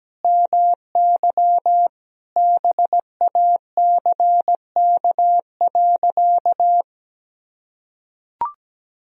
Morse code, heard as a keyed tone, about 17 wpm, 700 Hz. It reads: MY BACK.
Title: Sentences from Top 1000 Words - Review Part 3 - 17wpm